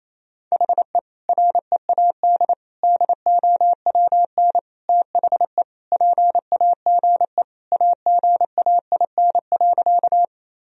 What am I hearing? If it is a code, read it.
Morse code, heard as a keyed tone, about 27 wpm, 700 Hz. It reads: HE READ DOWN THE PAGE AGAIN.
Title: Sentences from Top 1000 Words - Review Part 1 - 28wpm